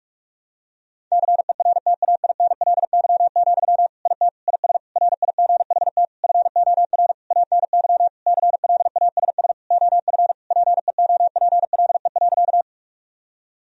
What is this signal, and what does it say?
Morse code, heard as a keyed tone, about 41 wpm, 700 Hz. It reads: CERTAINLY, IT IS RIGHT FOR ANY CLASS OF PEOPLE.
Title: Sentences from Top 1000 Words - Review Part 2 - 45wpm